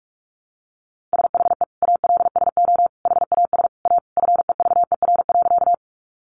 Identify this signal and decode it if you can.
Morse code, heard as a keyed tone, about 44 wpm, 700 Hz. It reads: SHE ALSO HAS A FEVER.
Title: Easy Phrases 45wpm